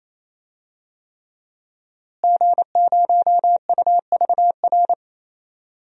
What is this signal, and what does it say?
Morse code, heard as a keyed tone, about 28 wpm, 700 Hz. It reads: G0UVR
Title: DXCC-100 Call Signs - 5 Characters - Repeated 3x - 28wpm